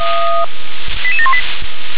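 Five-tone selective call (ZVEI), heard as quick runs of five tones, tones of about 70 ms. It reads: 88718
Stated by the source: Five-tone ZVEI selective call, SDR-received, address 88711.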